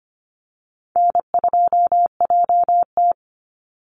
Morse code, heard as a keyed tone, about 25 wpm, 700 Hz. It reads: N2JT
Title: Easy Call Signs Repeated 3x - 25wpm